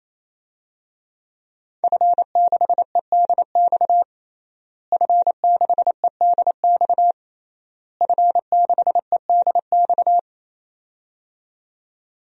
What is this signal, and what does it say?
Morse code, heard as a keyed tone, about 27 wpm, 700 Hz. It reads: F6EDX F6EDX F6EDX
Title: DXCC-100 Call Signs - 5 Characters - Repeated 3x - 28wpm